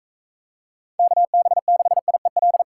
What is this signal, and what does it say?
Morse code, heard as a keyed tone, about 42 wpm, 700 Hz. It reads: KB6IEL